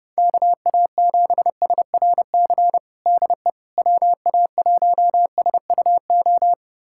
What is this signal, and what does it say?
Morse code, heard as a keyed tone, about 29 wpm, 700 Hz. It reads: KA7SRC DE WA1SUO
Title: USA Callsign DE Callsign - 6 Characters - 30wpm